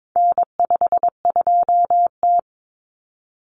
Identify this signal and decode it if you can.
Morse code, heard as a keyed tone, about 22 wpm, 700 Hz. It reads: N52T
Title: Common Cut Numbers - 4-Digit 22wpm